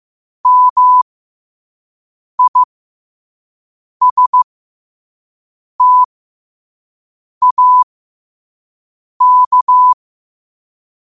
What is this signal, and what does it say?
Morse code, heard as a keyed tone, about 15 wpm, spaced as for 5 wpm, 1000 Hz. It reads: MISTAK
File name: qotdfiles / 5wpm